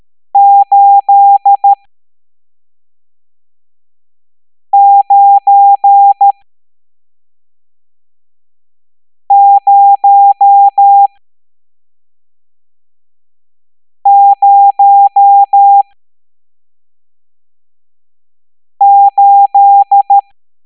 Morse code, heard as a keyed tone, about 13 wpm, spaced as for 3 wpm, 800 Hz. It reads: 89008
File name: hamdata / L11-EX01-67890